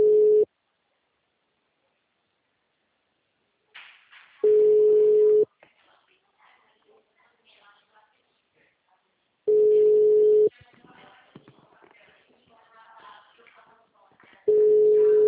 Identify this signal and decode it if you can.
Ringback tone (Europe, 425 Hz), with speech.
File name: out- -2002- - - .22915